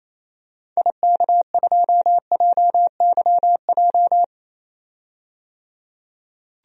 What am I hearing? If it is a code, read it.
Morse code, heard as a keyed tone, about 27 wpm, 700 Hz. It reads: IK2JYJ
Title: DXCC-100 Call Signs - 6 Characters - 28wpm